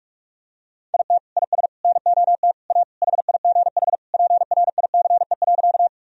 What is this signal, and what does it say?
Morse code, heard as a keyed tone, about 46 wpm, 700 Hz. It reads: IT IS NOT A HIGH PRICE.
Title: Sentences from Top 1000 Words - Review Part 4 - 45wpm